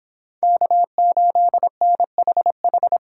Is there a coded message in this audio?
Yes: Morse code from 0.4 to 3.0 s, heard as a keyed tone, about 26 wpm, 700 Hz. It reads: K8NHH